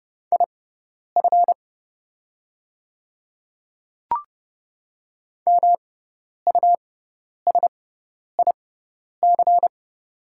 Morse code, heard as a keyed tone, about 29 wpm, spaced as for 10 wpm, 690 Hz. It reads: IF MUSIC
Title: 6x Character Spacing - Top 300 Words - 30wpm